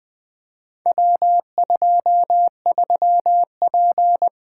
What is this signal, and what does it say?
Morse code, heard as a keyed tone, about 20 wpm, 700 Hz. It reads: W23P
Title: Four Letter-Number Combinations - 20wpm